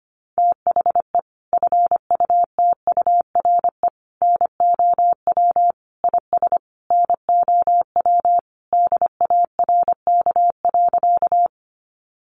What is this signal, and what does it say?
Morse code, heard as a keyed tone, about 25 wpm, 700 Hz. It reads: THE FUTURE NOW IS NOW DARK.
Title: Sentences from Top 700 Words 25wpm